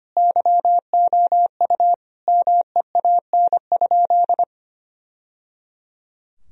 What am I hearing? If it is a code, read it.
Morse code, heard as a keyed tone, about 25 wpm, 700 Hz. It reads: YOU MEAN?
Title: Sentences from Top 1000 Words - Review Part 3 - 25wpm